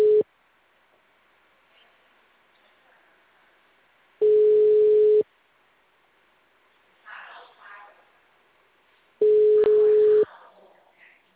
Ringback tone (Europe, 425 Hz).
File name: out- -2044- - - .72076